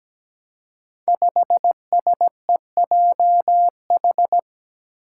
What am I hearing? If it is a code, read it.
Morse code, heard as a keyed tone, about 17 wpm, 700 Hz. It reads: 5SEJH
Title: Five Challenging Letter-Number Combinations 17wpm